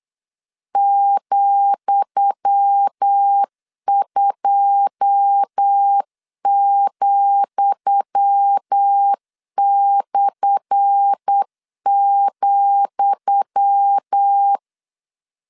Morse code, heard as a keyed tone, about 8 wpm, 800 Hz. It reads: ,2,/,
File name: LEZIONI / 10 SIGNS / 10D COMA 8wpm 3749